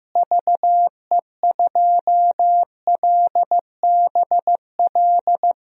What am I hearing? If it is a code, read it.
Morse code, heard as a keyed tone, about 15 wpm, 700 Hz. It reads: VE2LBL
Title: DXCC-100 Call Signs - 6 Characters - Repeated 3x - 15wpm